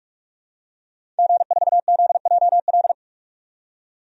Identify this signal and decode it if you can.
Morse code, heard as a keyed tone, about 46 wpm, 700 Hz. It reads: G4ZJL
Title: DXCC-100 Call Signs - 5 Characters - 45wpm